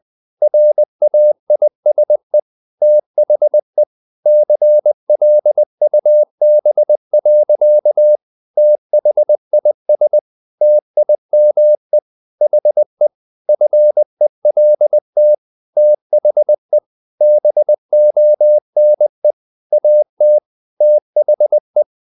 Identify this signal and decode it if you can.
Morse code, heard as a keyed tone, about 20 wpm, 600 Hz. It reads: RAISE THE CLUB. THIS TIME HE FELT THE BONE AT THE
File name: the old man and the sea part 2 20wpm